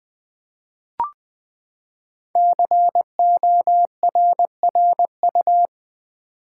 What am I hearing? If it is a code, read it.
Morse code, heard as a keyed tone, about 20 wpm, 700 Hz. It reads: CORRU
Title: Build A Word - 5 Letters At A Time - 10-Letter Words - 20wpm